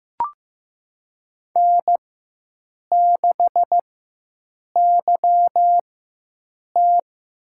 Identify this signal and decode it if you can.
Morse code, heard as a keyed tone, about 15 wpm, spaced as for 7 wpm, 700 Hz. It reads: N6YT
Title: USA Call Signs - 4x Character Spacing - 15wpm